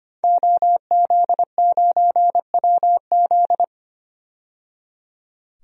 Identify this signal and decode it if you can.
Morse code, heard as a keyed tone, about 25 wpm, 700 Hz. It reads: OZ9WZ
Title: DXCC-100 Call Signs - 5 Characters - 25wpm